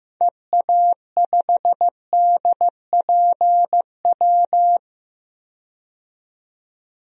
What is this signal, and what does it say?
Morse code, heard as a keyed tone, about 15 wpm, 700 Hz. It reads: EA5DPW
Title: DXCC-100 Call Signs - 6 Characters - 15wpm